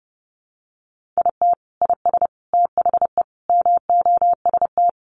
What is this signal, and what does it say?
Morse code, heard as a keyed tone, about 30 wpm, 700 Hz. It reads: IT IS THE MOST
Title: Sentences from Top 1500 Words 30wpm